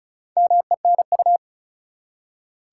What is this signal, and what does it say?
Morse code, heard as a keyed tone, about 36 wpm, 700 Hz. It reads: MENU
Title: Four Letter Words - 35wpm